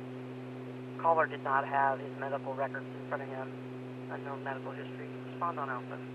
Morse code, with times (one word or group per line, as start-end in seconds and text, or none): none